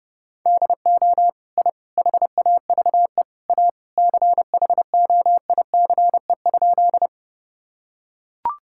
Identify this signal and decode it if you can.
Morse code, heard as a keyed tone, about 30 wpm, 700 Hz. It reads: DO I HAVE A CHOICE?
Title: Sentences from Top 1000 Words - Review Part 6 - 30wpm